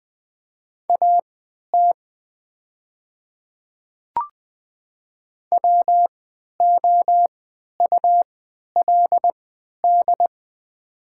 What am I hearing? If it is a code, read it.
Morse code, heard as a keyed tone, about 20 wpm, spaced as for 11 wpm, 700 Hz. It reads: AT WOULD
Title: 3x Character Spacing - Top 100 Words - 20wpm